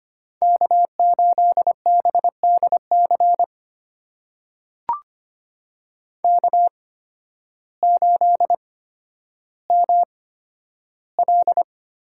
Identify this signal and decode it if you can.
Morse code, heard as a keyed tone, about 25 wpm, spaced as for 12 wpm, 700 Hz. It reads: K8BDC K8ML
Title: USA Call Signs - 5 Characters - 8x Character Spacing - 25wpm